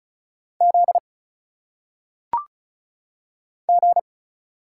Morse code, heard as a keyed tone, about 36 wpm, 700 Hz. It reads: Z G
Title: Single Letters - Phonetic Pronunciation 35wpm